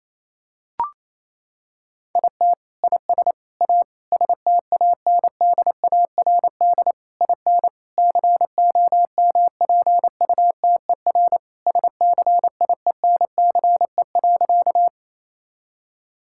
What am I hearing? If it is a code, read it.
Morse code, heard as a keyed tone, about 28 wpm, 700 Hz. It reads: IT IS A STANDARD IN COMPUTER SCIENCE.
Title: Sentences from Top 1000 Words - Review Part 6 - 28wpm